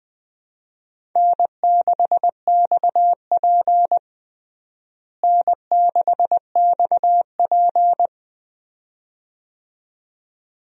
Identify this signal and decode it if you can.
Morse code, heard as a keyed tone, about 21 wpm, 700 Hz. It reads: N6XP N6XP
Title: USA Call Signs - 4 Characters - Repeated 2x - 20wpm